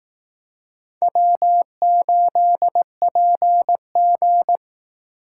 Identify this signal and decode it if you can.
Morse code, heard as a keyed tone, about 18 wpm, 700 Hz. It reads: W8PG
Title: Speed Racing - USA 1x2 Call Signs 15wpm